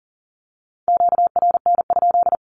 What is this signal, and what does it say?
Morse code, heard as a keyed tone, about 40 wpm, 700 Hz. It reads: QRN?
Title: Q-Codes Repeated 3x 40wpm